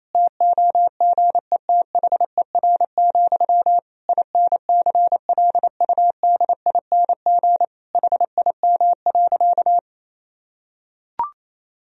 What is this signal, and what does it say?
Morse code, heard as a keyed tone, about 28 wpm, 700 Hz. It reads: TOGETHER, INCLUDING HIM.
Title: Sentences from Top 1000 Words - Review Part 4 - 28wpm